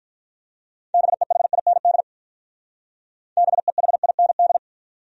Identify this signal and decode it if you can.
Morse code, heard as a keyed tone, about 46 wpm, 700 Hz. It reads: BEHIND BEHIND